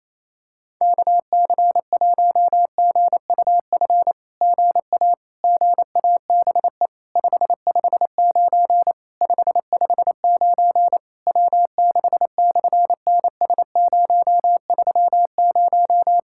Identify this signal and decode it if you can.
Morse code, heard as a keyed tone, about 27 wpm, 700 Hz. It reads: KC1GUF GA GABE 559 559 W6/NS030
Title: SOTA QSO Practice 28wpm